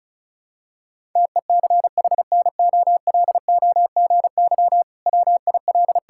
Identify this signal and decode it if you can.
Morse code, heard as a keyed tone, about 36 wpm, 700 Hz. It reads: TECHNOLOGY WIL
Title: Sentences from Top 1000 Words - Review Part 5 - 35wpm